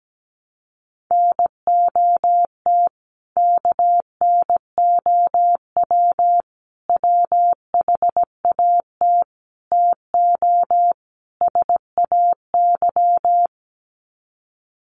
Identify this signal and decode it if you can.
Morse code, heard as a keyed tone, about 17 wpm, 700 Hz. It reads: NOT KNOW WHAT TO SAY
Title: Sets of 5 Words 17wpm